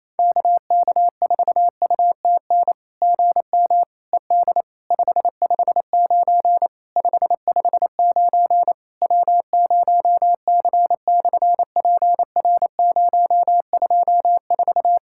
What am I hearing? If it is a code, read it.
Morse code, heard as a keyed tone, about 28 wpm, 700 Hz. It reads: KK4UTN GM ED 559 559 W0C/PR024
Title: SOTA QSO Practice 28wpm